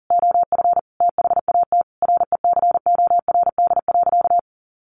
Morse code, heard as a keyed tone, about 37 wpm, 700 Hz. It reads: OF THAT RECORD.
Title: Sentences from Top 1400 Words 40wpm